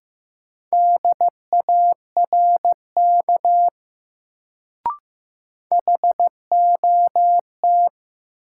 Morse code, heard as a keyed tone, about 15 wpm, 700 Hz. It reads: DARK HOT
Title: Sentences from Top 500 Words - New Word Drill - 15wpm